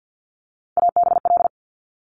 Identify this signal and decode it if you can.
Morse code, heard as a keyed tone, about 50 wpm, 700 Hz. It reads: ABL
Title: Top Three Letter Combinations in Words 50wpm